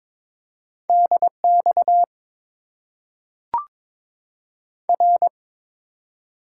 Morse code, heard as a keyed tone, about 22 wpm, 700 Hz. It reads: DX R
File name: Top 90 QSO Elements - 8x Character Spacing - 22wpm